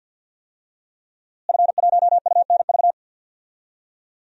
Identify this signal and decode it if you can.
Morse code, heard as a keyed tone, about 52 wpm, 700 Hz. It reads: F1UNV
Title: DXCC-100 Call Signs - 5 Characters - Repeated 3x - 50wpm